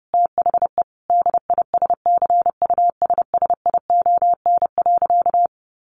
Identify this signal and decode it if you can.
Morse code, heard as a keyed tone, about 30 wpm, 700 Hz. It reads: THE DISCUSSION.
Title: Sentences from Top 1100 Words 30wpm